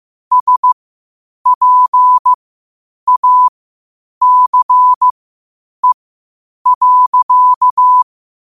Morse code, heard as a keyed tone, about 15 wpm, spaced as for 8 wpm, 1000 Hz. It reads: SPACE.
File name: qotdfiles / 8wpm